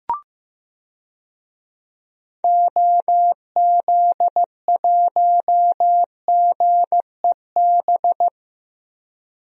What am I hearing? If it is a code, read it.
Morse code, heard as a keyed tone, about 15 wpm, 700 Hz. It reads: OZ1GEB
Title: DXCC-100 Call Signs - 6 Characters - 15wpm